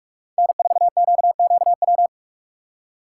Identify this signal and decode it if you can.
Morse code, heard as a keyed tone, about 46 wpm, 700 Hz. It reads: N4QQW
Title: USA Call Signs - 5 Characters - 2x Character Spacing - 45wpm